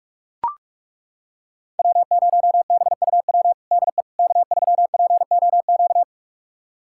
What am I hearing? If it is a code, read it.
Morse code, heard as a keyed tone, about 41 wpm, 700 Hz. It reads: W0BUW DE K3POQ
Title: USA Callsign DE Callsign - 5 Characters - 45wpm